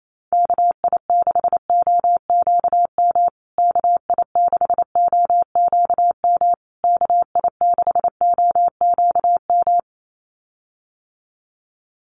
Morse code, heard as a keyed tone, about 28 wpm, 700 Hz. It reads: KI6OQM KI6OQM KI6OQM
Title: USA Call Signs Repeated 3x 28wpm